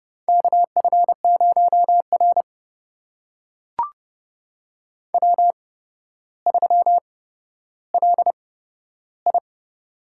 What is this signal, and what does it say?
Morse code, heard as a keyed tone, about 30 wpm, spaced as for 13 wpm, 700 Hz. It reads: KF0R W3LI
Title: USA Call Signs - 4 Characters - 8x Character Spacing - 30wpm